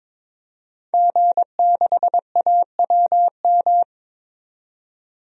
Morse code, heard as a keyed tone, about 21 wpm, 700 Hz. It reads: G6AWM
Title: DXCC-100 Call Signs - 5 Characters - 22wpm